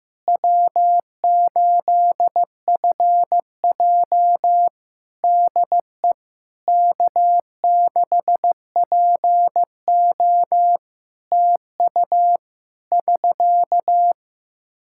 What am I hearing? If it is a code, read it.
Morse code, heard as a keyed tone, about 15 wpm, 700 Hz. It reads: W8FJ DE K6PO TU <SK>
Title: Practice QSO - Exchange 4 - Ending - 15wpm